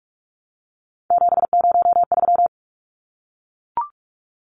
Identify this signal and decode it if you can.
Morse code, heard as a keyed tone, about 44 wpm, 700 Hz. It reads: 703
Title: Numbers - 3-Digit 45wpm